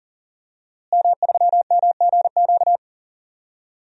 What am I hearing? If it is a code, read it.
Morse code, heard as a keyed tone, about 40 wpm, 700 Hz. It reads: M3MGQ
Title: DXCC-100 Call Signs - 5 Characters - Repeated 3x - 40wpm